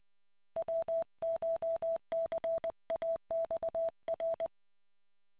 Morse code, heard as a keyed tone, about 20 wpm, 660 Hz. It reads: WCHCAXR